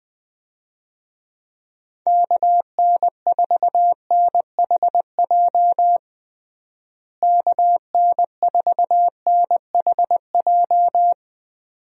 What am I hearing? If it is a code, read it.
Morse code, heard as a keyed tone, about 21 wpm, 700 Hz. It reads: KN4NHJ KN4NHJ
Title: USA Call Signs - 6 Characters - Repeated 2x - 20wpm